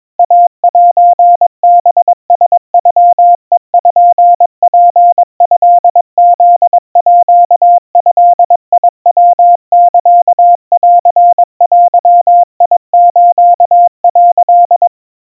Morse code (Japanese Wabun code), heard as a keyed tone, about 22 wpm, 700 Hz. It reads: イセハラノペツトフードヤサンデス」
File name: 80 cpm 700 Hz Crispy